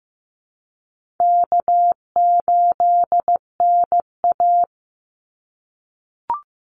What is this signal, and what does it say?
Morse code, heard as a keyed tone, about 15 wpm, 700 Hz. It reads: K8NA
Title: Easy Call Signs - 15wpm